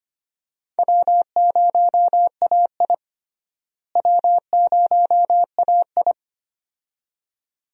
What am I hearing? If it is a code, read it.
Morse code, heard as a keyed tone, about 25 wpm, 700 Hz. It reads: W0AI W0AI
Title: USA Call Signs - 4 Characters - Repeated 2x - 25wpm